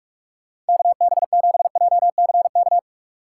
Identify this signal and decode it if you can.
Morse code, heard as a keyed tone, about 58 wpm, 700 Hz. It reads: KB7JCK